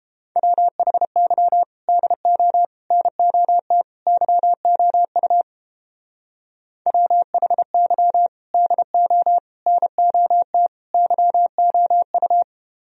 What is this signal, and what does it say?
Morse code, heard as a keyed tone, about 29 wpm, 700 Hz. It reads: WHY DO NOT YOU WHY DO NOT YOU